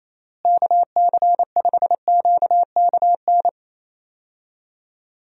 Morse code, heard as a keyed tone, about 28 wpm, 700 Hz. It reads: KC5QKN
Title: USA Call Signs - 6 Characters - Repeated 3x - 28wpm